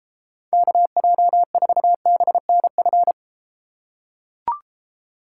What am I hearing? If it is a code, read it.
Morse code, heard as a keyed tone, about 33 wpm, 700 Hz. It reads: KJ4BNF